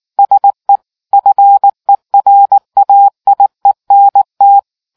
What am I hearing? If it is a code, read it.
Morse code, heard as a keyed tone, about 19 wpm, 800 Hz. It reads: SE FERAIENT